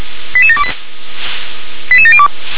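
Five-tone selective call (ZVEI), heard as quick runs of five tones, tones of about 70 ms. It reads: 88728 88702
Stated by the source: ZVEI five-tone call 88730 received by SDR.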